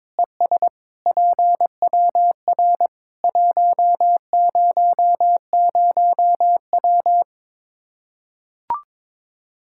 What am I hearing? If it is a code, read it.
Morse code, heard as a keyed tone, about 21 wpm, 700 Hz. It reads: ES PWR 100W
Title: Practice QSO - Exchange 2 - Info - 22wpm